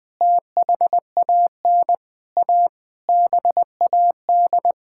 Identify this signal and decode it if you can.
Morse code, heard as a keyed tone, about 21 wpm, 700 Hz. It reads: THAN A BAD
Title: Sentences with 9-Word Phrases - 20wpm